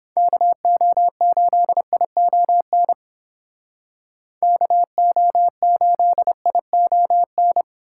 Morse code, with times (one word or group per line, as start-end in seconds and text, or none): 0.2-2.9 KO8ION
4.4-7.6 KO8ION